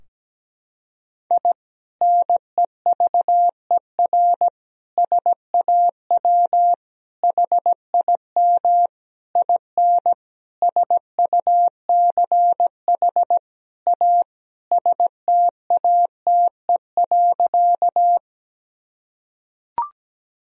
Morse code, heard as a keyed tone, about 17 wpm, 700 Hz. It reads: I NEVER SAW HIM IN SUCH A STATE.
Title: Sentences from Top 1000 Words - Review Part 2 - 17wpm